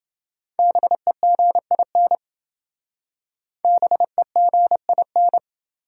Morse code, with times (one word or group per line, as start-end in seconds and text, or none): 0.6-2.1 BEGIN
3.6-5.4 BEGIN